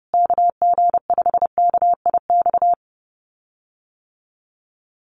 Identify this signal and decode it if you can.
Morse code, heard as a keyed tone, about 28 wpm, 700 Hz. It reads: KG5KIX
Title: USA Call Signs Repeated 2x 30wpm